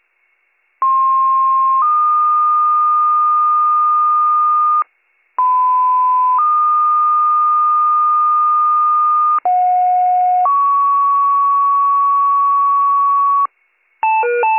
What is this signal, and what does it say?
Two-tone page, heard as a one-second tone, then a three-second tone, about 1060 then 1220 Hz, about 1010 then 1220 Hz, about 730 then 1120 Hz. It is heard from 0.8 to 13.5 s.